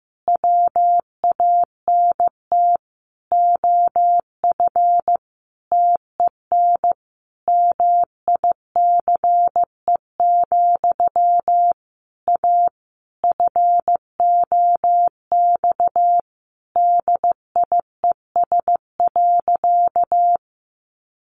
Morse code, heard as a keyed tone, about 15 wpm, 700 Hz. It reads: WANT OF TEN MICE, A FOX DIES.
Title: A Sound Of Thunder 15wpm